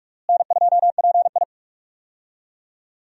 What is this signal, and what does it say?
Morse code, heard as a keyed tone, about 46 wpm, 700 Hz. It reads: N2PI
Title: USA Call Signs - 4 Characters - Repeated 3x - 45wpm